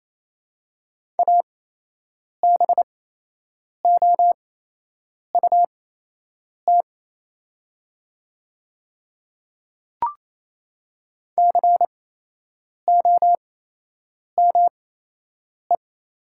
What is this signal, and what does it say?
Morse code, heard as a keyed tone, about 28 wpm, spaced as for 8 wpm, 700 Hz. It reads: ABOUT COME
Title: 8x Character Spacing - Top 500 Words - 28wpm